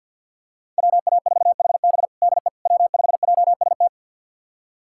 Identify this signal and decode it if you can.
Morse code, heard as a keyed tone, about 49 wpm, 700 Hz. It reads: WA4HB DE W5JST
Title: USA Callsign DE Callsign - 5 Characters - 50wpm